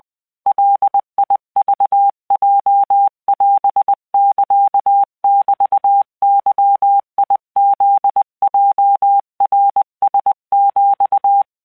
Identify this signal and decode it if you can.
Morse code (Japanese Wabun code), heard as a keyed tone, about 20 wpm, 800 Hz. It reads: ガクヲオサメゲフヲナラヒ
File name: keuikuchiyokugo20wptan